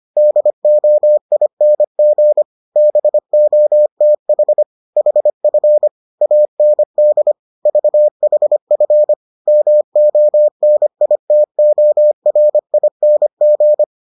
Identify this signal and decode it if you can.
Morse code, heard as a keyed tone, about 25 wpm, 600 Hz. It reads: DOING BOTH HF AND VHF MONITORING